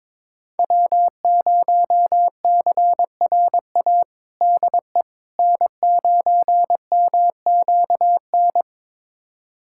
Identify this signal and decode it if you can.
Morse code, heard as a keyed tone, about 22 wpm, 700 Hz. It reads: W0CRA DE N9MQN
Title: USA Callsign DE Callsign - 5 Characters - 22wpm